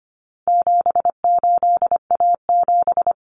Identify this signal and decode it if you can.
Morse code, heard as a keyed tone, about 25 wpm, 700 Hz. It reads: 78A7
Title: Common Cut Numbers - 4-Digit 25wpm